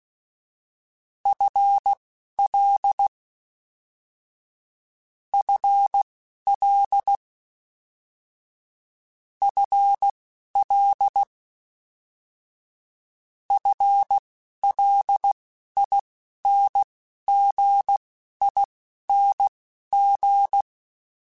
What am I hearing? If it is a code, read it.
Morse code, heard as a keyed tone, about 16 wpm, spaced as for 12 wpm, 800 Hz. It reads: FL FL FL FLINGING
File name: FL Prefixes 3x2 Twice @10